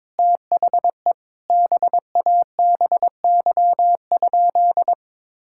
Morse code, heard as a keyed tone, about 22 wpm, 700 Hz. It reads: THE BABY?
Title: Sentences from Top 1600 Words 22wpm